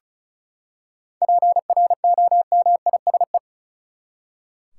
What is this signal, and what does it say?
Morse code, heard as a keyed tone, about 34 wpm, 700 Hz. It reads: PROMISE
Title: Seven Letter Words - 35wpm